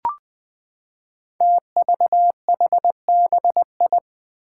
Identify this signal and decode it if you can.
Morse code, heard as a keyed tone, about 20 wpm, 700 Hz. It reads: TVHBI
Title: Five Challenging Letter-Number Combinations 20wpm